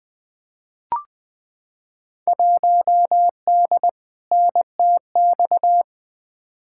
Morse code, heard as a keyed tone, about 21 wpm, 700 Hz. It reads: 1D NTX
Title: Field Day Exchanges - 20wpm